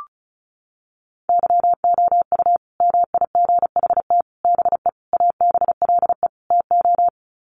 Morse code, heard as a keyed tone, about 34 wpm, 700 Hz. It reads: YOU MIGHT BE ABLE TO
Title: Sets of 5 Words Encore2 35wpm